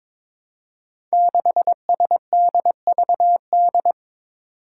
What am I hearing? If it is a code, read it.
Morse code, heard as a keyed tone, about 22 wpm, 700 Hz. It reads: 6SDVD
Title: Five Challenging Letter-Number Combinations 22wpm